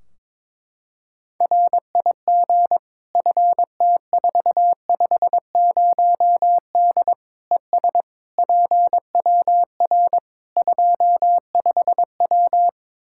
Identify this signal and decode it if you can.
Morse code, heard as a keyed tone, about 22 wpm, 700 Hz. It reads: RIG FT450D ES PWR 25W
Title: K1USN SST QSO Practice 22wpm